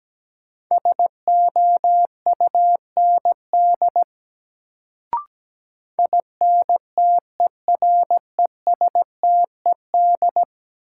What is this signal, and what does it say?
Morse code, heard as a keyed tone, about 17 wpm, 700 Hz. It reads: SOUND INTERESTED
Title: Sentences from Top 900 Words - New Word Drill - 17wpm